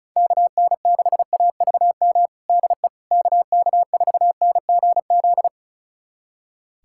Morse code, heard as a keyed tone, about 36 wpm, 700 Hz. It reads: KN6AVM DE KK4NGZ
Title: USA Callsign DE Callsign 35wpm